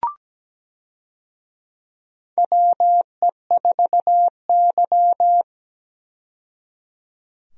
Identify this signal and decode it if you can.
Morse code, heard as a keyed tone, about 17 wpm, 700 Hz. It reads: WE4Y
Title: USA Call Signs - 4 Characters - 17wpm